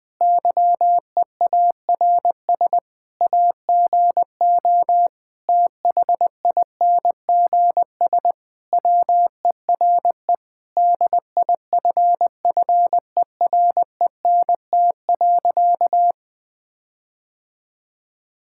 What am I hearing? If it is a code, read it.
Morse code, heard as a keyed tone, about 20 wpm, 700 Hz. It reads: YEARS AGO THINGS WERE DIFFERENT.
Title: Sentences with 5-Word Phrases - 20wpm